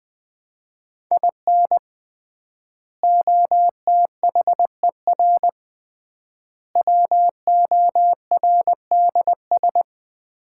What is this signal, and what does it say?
Morse code, heard as a keyed tone, about 20 wpm, 700 Hz. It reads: IN OTHER WORDS
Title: 3x Word-Spacing - Sets of 3 Words - 20wpm